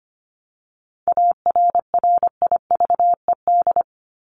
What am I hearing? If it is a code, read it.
Morse code, heard as a keyed tone, about 25 wpm, 700 Hz. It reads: ARRIVED
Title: ED Suffix - Top 100 Words with Base Words 25wpm